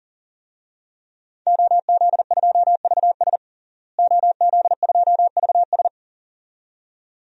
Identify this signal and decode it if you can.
Morse code, heard as a keyed tone, about 40 wpm, 690 Hz. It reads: OZ2VS OZ2VS
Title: DXCC-100 Call Signs - 5 Characters - Repeated 2x - 40wpm